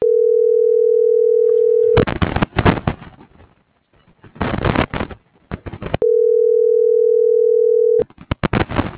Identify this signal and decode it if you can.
Ringback tone (North America), 440 + 480 Hz.